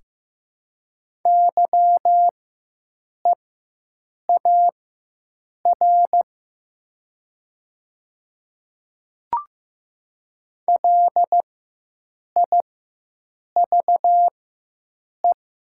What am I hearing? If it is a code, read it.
Morse code, heard as a keyed tone, about 15 wpm, spaced as for 7 wpm, 700 Hz. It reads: YEAR LIVE
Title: 4x Character Spacing - Top 200 Words - 15wpm